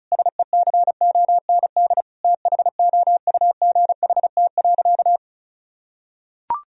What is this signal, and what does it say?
Morse code, heard as a keyed tone, about 36 wpm, 700 Hz. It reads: SECOND THOUGHT.
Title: Sentences from Top 1000 Words - Review Part 2 - 35wpm